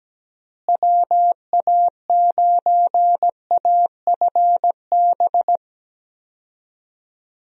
Morse code, heard as a keyed tone, about 17 wpm, 700 Hz. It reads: WA9AFB
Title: USA Call Signs - 6 Characters - Repeated 3x - 17wpm